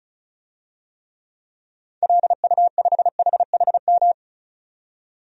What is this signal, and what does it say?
Morse code, heard as a keyed tone, about 36 wpm, 700 Hz. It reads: LU5HHM
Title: DXCC-100 Call Signs - 6 Characters - 35wpm